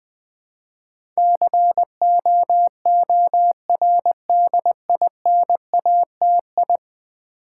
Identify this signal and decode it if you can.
Morse code, heard as a keyed tone, about 20 wpm, 700 Hz. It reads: COORDINATI
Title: Build A Word - 5 Letters At A Time - 12-Letter Words - 20wpm